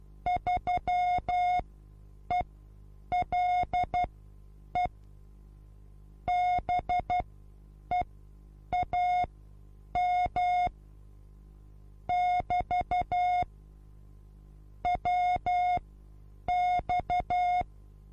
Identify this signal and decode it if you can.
Morse code, heard as a keyed tone, about 12 wpm, spaced as for 8 wpm, 720 Hz. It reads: 3ELE BEAM <BT> WX
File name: Onlinelehrgang / morsen / DJ4UF Morsen23